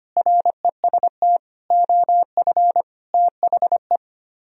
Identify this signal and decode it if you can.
Morse code, heard as a keyed tone, about 32 wpm, 700 Hz. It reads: REST OF THE